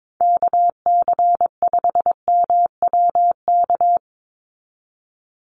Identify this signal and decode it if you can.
Morse code, heard as a keyed tone, about 22 wpm, 700 Hz. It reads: KC5MWK
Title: USA Call Signs 22wpm